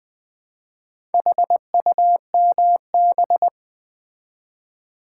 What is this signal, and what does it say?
Morse code, heard as a keyed tone, about 20 wpm, 700 Hz. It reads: HUMB
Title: Build A Word - 2 Letters At A Time - 6-Letter Words - 20wpm